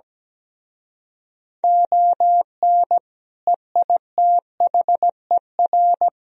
Morse code, heard as a keyed tone, about 17 wpm, 700 Hz. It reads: ON EITHER
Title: Speed-Racing - Sets of 5 Words 17wpm